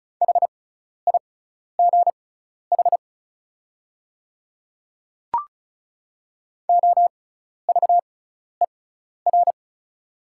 Morse code, heard as a keyed tone, about 36 wpm, spaced as for 12 wpm, 700 Hz. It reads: HIGH OVER
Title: 6x Character Spacing - Top 200 Words - 35wpm